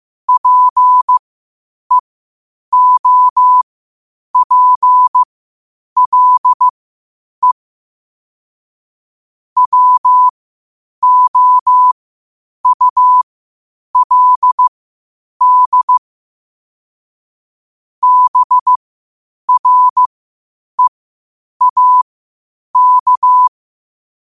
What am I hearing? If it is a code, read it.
Morse code, heard as a keyed tone, about 15 wpm, spaced as for 8 wpm, 1000 Hz. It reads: PEOPLE WOULD BREAK
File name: 8wpm